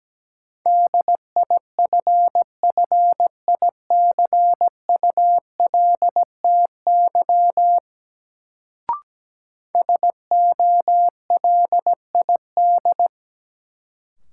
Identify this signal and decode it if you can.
Morse code, heard as a keyed tone, about 17 wpm, 700 Hz. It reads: DIFFICULTY SOLID
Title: Sentences from Top 1500 Words - New Word Drill Alt - 17wpm